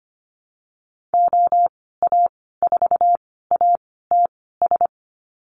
Morse code, heard as a keyed tone, about 25 wpm, spaced as for 16 wpm, 700 Hz. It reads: OA4ATS
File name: Lesson 15g - Number 5 - Call Signs 25wpm